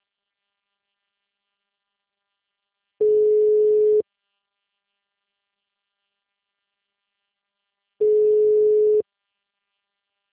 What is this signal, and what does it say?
Ringback tone (Europe, 425 Hz).